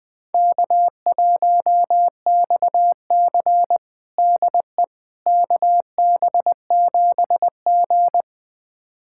Morse code, heard as a keyed tone, about 20 wpm, 700 Hz. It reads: K1XC DE KB7G
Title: USA Callsign DE Callsign - 4 Characters - 20wpm